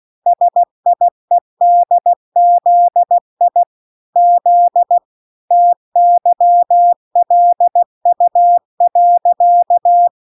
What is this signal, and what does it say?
Morse code, heard as a keyed tone, about 16 wpm, 700 Hz. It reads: SIEDZI Z TYLU.